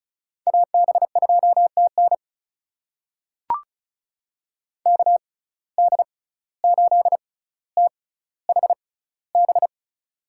Morse code, heard as a keyed tone, about 36 wpm, spaced as for 17 wpm, 700 Hz. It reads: AB2TN KD8THB
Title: USA Call Signs - 6x Character Spacing - 35wpm